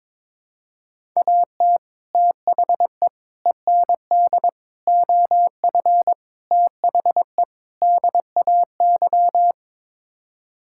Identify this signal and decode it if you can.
Morse code, heard as a keyed tone, about 22 wpm, 700 Hz. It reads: AT THE END OF THE DAY
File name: practice / 22wpm / Sets of 6 Words - 22wpm